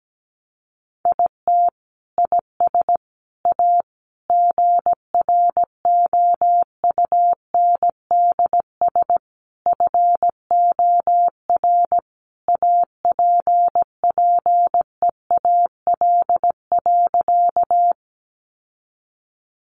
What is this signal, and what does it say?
Morse code, heard as a keyed tone, about 17 wpm, 700 Hz. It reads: IT IS A GROUNDS FOR APPEAL.
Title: Sentences from Top 1300 Words 17wpm